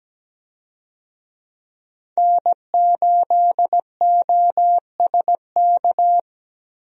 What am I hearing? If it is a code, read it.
Morse code, heard as a keyed tone, about 17 wpm, 700 Hz. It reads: N8OSK
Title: USA Call Signs - 5 Characters - Repeated 2x - 17wpm